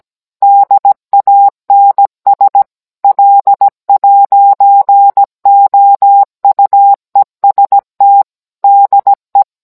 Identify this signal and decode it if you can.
Morse code, heard as a keyed tone, about 17 wpm, 800 Hz. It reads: DANS L'OUEST DE